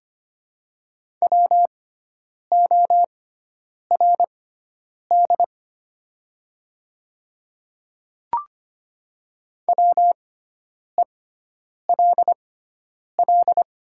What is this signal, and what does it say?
Morse code, heard as a keyed tone, about 25 wpm, spaced as for 9 wpm, 700 Hz. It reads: WORD WELL
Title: 6x Character Spacing - Top 200 Words - 25wpm